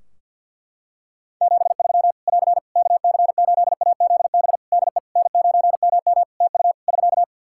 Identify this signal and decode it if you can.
Morse code, heard as a keyed tone, about 50 wpm, 700 Hz. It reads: 73 <AR> KC8AZB DE N9MM TU <SK>
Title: K1USN SST QSO Practice 50wpm